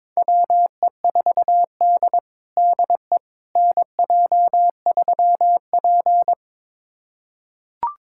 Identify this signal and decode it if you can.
Morse code, heard as a keyed tone, about 21 wpm, 700 Hz. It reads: WE4D DE NJ3P
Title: USA Callsign DE Callsign - 4 Characters - 22wpm